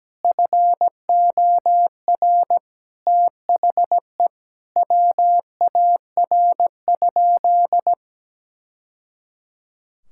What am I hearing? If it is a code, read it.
Morse code, heard as a keyed tone, about 17 wpm, 700 Hz. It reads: FOR THE WAR?
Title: Sentences from Top 1000 Words - Review Part 5 - 17wpm